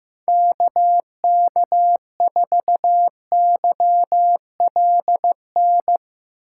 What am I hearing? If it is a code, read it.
Morse code, heard as a keyed tone, about 15 wpm, 700 Hz. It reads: KK4YLN
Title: USA Call Signs - 6 Characters - 2x Character Spacing - 15wpm